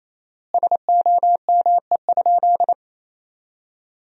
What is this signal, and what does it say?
Morse code, heard as a keyed tone, about 28 wpm, 700 Hz. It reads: SOME?
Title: 4x Word-Spacing - Sentences from Top 100 Words - 28wpm